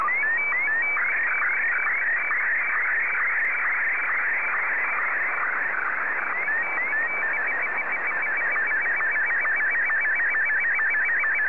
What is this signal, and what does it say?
SSTV (slow-scan television), Martin 1.
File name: sstv / M1-VK2OY